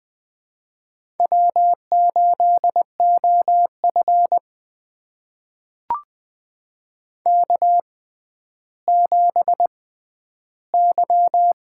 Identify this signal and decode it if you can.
Morse code, heard as a keyed tone, about 20 wpm, spaced as for 11 wpm, 700 Hz. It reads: W8OF K7Y
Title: USA Call Signs - 4 Characters - 6x Character Spacing - 20wpm